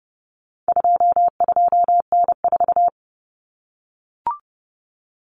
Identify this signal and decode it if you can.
Morse code, heard as a keyed tone, about 30 wpm, 700 Hz. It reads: 22N4
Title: Common Cut Numbers - 4-Digit 30wpm